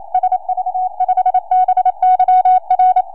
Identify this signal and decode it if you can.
Morse code, heard as a keyed tone, about 27 wpm, 740 Hz. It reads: SV5BYR
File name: 4m